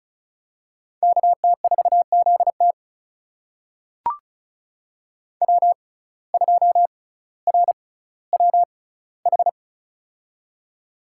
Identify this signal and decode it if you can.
Morse code, heard as a keyed tone, about 36 wpm, spaced as for 18 wpm, 700 Hz. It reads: KT4ZT W2RWH
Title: USA Call Signs - 5 Characters - 6x Character Spacing - 35wpm